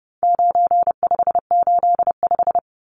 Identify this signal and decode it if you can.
Morse code, heard as a keyed tone, about 30 wpm, 700 Hz. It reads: 9585
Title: Numbers - 4-Digit 30wpm